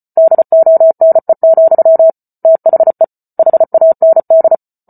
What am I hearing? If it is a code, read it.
Morse code, heard as a keyed tone, about 34 wpm, 650 Hz. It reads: DONE, THE HAND